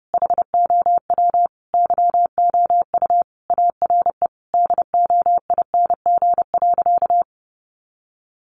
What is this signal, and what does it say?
Morse code, heard as a keyed tone, about 30 wpm, 700 Hz. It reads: HOW YOU ARE DOING.
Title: Sentences from Top 1400 Words 30wpm